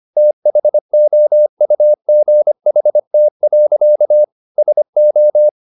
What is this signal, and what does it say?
Morse code, heard as a keyed tone, about 25 wpm, 600 Hz. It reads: THOUGHT. SO